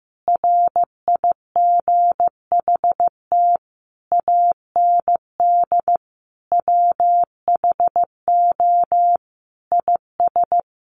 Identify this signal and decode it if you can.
Morse code, heard as a keyed tone, about 15 wpm, 700 Hz. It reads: RIGHT AND WHO IS
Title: Sentences from Top 900 Words 15wpm